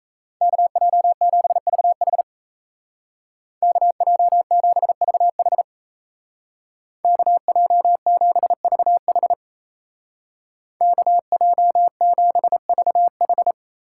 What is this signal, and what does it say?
Morse code, heard as a keyed tone, about 33 wpm, 700 Hz. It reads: KJ7VH KJ7VH KJ7VH KJ7VH
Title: Speed Racing - USA 2x2 Call Signs 28wpm